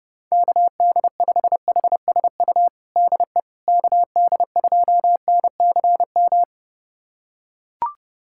Morse code, heard as a keyed tone, about 30 wpm, 700 Hz. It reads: KD5HSU DE KD2NCM
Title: USA Callsign DE Callsign - 6 Characters - 30wpm